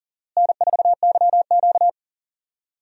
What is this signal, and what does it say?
Morse code, heard as a keyed tone, about 40 wpm, 700 Hz. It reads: N4YQ
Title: USA Call Signs - 4 Characters - Repeated 2x - 40wpm